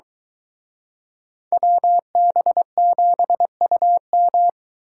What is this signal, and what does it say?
Morse code, heard as a keyed tone, about 23 wpm, 700 Hz. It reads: WB7UM